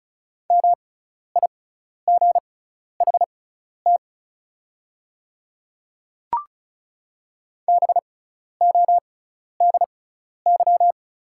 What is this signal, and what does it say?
Morse code, heard as a keyed tone, about 34 wpm, spaced as for 12 wpm, 690 Hz. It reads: MIGHT BODY
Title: 6x Character Spacing - Top 500 Words - 35wpm